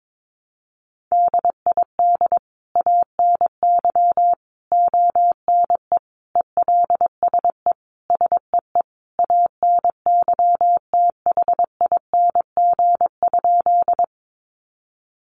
Morse code, heard as a keyed tone, about 22 wpm, 700 Hz. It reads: DID ANY ONE ELSE SEE ANYTHING?
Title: Sentences from Top 1000 Words 22wpm